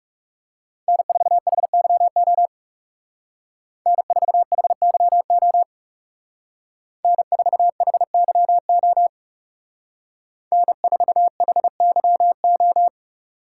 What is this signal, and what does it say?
Morse code, heard as a keyed tone, about 36 wpm, 700 Hz. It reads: N4HYO N4HYO N4HYO N4HYO
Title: Speed Racing - USA 1x3 Call Signs 30wpm